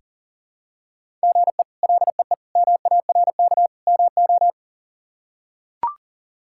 Morse code, heard as a keyed tone, about 40 wpm, 700 Hz. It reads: GE LEE MARK MO
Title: K1USN SST QSO Practice - 40wpm